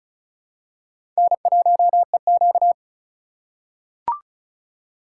Morse code, heard as a keyed tone, about 34 wpm, 700 Hz. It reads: N1EQ
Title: USA Call Signs - 3x Character Spacing - 35wpm